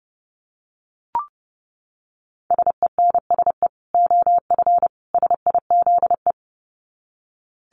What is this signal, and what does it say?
Morse code, heard as a keyed tone, about 30 wpm, 700 Hz. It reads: SENSE OF SIZE
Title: Dit Challenge 30wpm